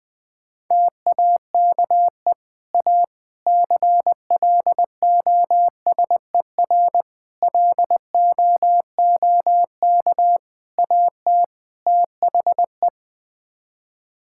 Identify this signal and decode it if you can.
Morse code, heard as a keyed tone, about 20 wpm, 700 Hz. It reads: TAKE A CLOSER LOOK AT THE
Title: Sets of 6 Words - 20wpm